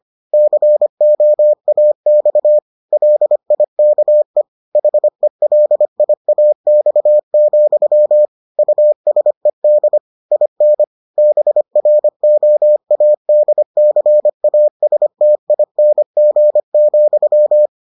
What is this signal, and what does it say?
Morse code, heard as a keyed tone, about 25 wpm, 600 Hz. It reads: COAX LIKE HELIAX, USED IN BROADCASTING,